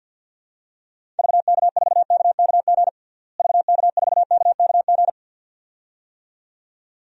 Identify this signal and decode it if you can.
Morse code, heard as a keyed tone, about 49 wpm, 700 Hz. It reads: VK4KKG VK4KKG
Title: DXCC-100 Call Signs - 6 Characters - Repeated 2x - 50wpm